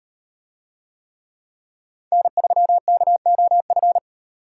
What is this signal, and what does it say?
Morse code, heard as a keyed tone, about 38 wpm, 700 Hz. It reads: N3KOF